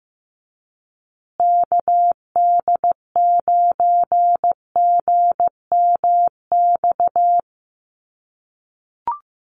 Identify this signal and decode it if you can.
Morse code, heard as a keyed tone, about 15 wpm, 700 Hz. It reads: KD9GMX
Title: USA Call Signs Repeated 3x 15wpm